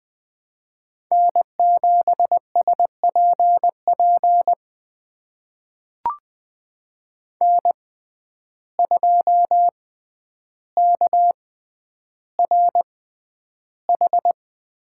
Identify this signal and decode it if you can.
Morse code, heard as a keyed tone, about 21 wpm, spaced as for 10 wpm, 700 Hz. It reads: N7SPP N2KRH
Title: USA Call Signs - 6x Character Spacing - 20wpm